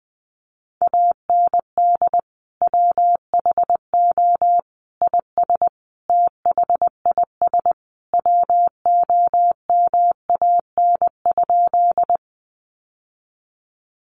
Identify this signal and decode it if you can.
Morse code, heard as a keyed tone, about 19 wpm, 700 Hz. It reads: AND WHO IS THIS WOMAN?
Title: Sentences from Top 1100 Words 20wpm